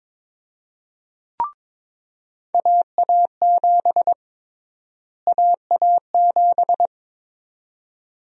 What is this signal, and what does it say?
Morse code, heard as a keyed tone, about 22 wpm, 700 Hz. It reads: AA7 AA7
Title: USA Call Sign Prefixes - Repeated 2x - 22wpm